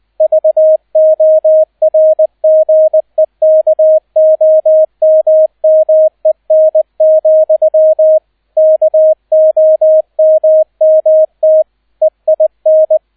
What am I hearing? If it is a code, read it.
Morse code, heard as a keyed tone, about 20 wpm, 630 Hz, with radio noise. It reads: VORGEKOMMEN, KOMMT EIN